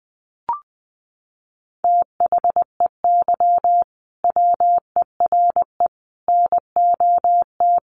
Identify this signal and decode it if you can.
Morse code, heard as a keyed tone, about 20 wpm, 700 Hz. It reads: THEY WERE NOT
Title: Sentences from Top 1000 Words 20wpm